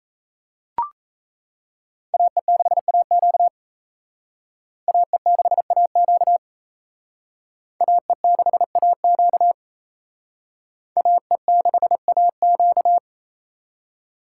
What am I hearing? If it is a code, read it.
Morse code, heard as a keyed tone, about 34 wpm, 700 Hz. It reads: AE6AQ AE6AQ AE6AQ AE6AQ
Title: Speed Racing - USA 2x2 Call Signs 28wpm